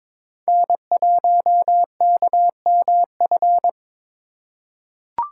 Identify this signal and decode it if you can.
Morse code, heard as a keyed tone, about 22 wpm, 700 Hz. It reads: N1KMF
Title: USA Call Signs - 3x Character Spacing - 22wpm